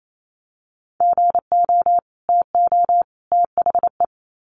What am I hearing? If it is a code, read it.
Morse code, heard as a keyed tone, about 28 wpm, 700 Hz. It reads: GO TO THE
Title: Sets of 3 Words 28wpm